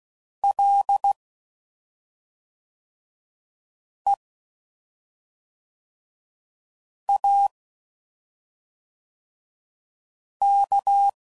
Morse code, heard as a keyed tone, about 16 wpm, 800 Hz. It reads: L E A K
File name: Suffixes / Special ING's / SlowToFaster ING's